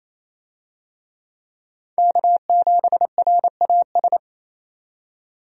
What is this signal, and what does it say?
Morse code, heard as a keyed tone, about 28 wpm, 700 Hz. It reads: K7RAS
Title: USA Call Signs - 5 Characters - 28wpm